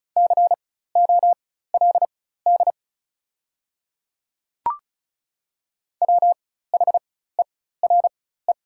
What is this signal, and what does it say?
Morse code, heard as a keyed tone, about 34 wpm, spaced as for 16 wpm, 690 Hz. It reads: COLD WHERE
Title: 4x Character Spacing - Top 500 Words - 35wpm